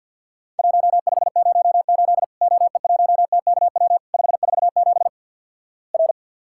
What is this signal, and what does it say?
Morse code, heard as a keyed tone, about 50 wpm, 700 Hz. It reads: 1508 OE1TKW 547 R
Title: Worked All Europe DX Contest - QTC Exchange - 50wpm